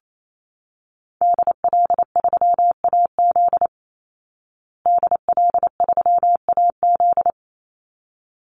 Morse code, heard as a keyed tone, about 28 wpm, 700 Hz. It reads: DL3AZ DL3AZ
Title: Call Signs Repeated 3x - 28wpm